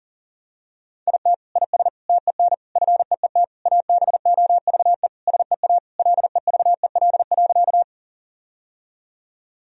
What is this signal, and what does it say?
Morse code, heard as a keyed tone, about 40 wpm, 700 Hz. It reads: IT IS TEN FEET ABOVE SEA LEVEL.
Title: Sentences from Top 1000 Words - Review Part 4 - 40wpm